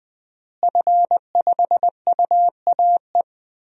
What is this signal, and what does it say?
Morse code, heard as a keyed tone, about 20 wpm, 700 Hz. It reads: F5UAE
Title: DXCC-100 Call Signs - 5 Characters - Repeated 3x - 20wpm